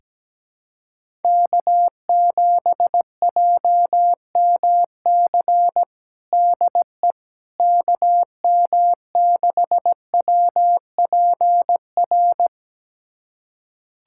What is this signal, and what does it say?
Morse code, heard as a keyed tone, about 17 wpm, 700 Hz. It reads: K7JMC DE KM6WPR
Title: USA Callsign DE Callsign 17wpm